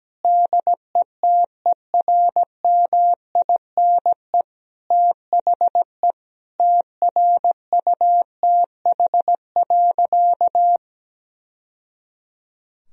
Morse code, heard as a keyed tone, about 17 wpm, 700 Hz. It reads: DETERMINE THE TRUTH.
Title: Sentences from Top 1000 Words - Review Part 4 - 17wpm